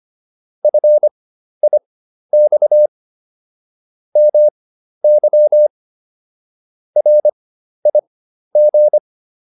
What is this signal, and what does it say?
Morse code, heard as a keyed tone, about 25 wpm, spaced as for 12 wpm, 600 Hz. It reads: FIX MY RIG